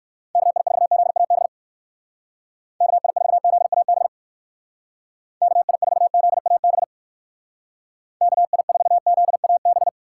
Morse code, heard as a keyed tone, about 59 wpm, 700 Hz. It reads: KI4ZAB KI4ZAB KI4ZAB KI4ZAB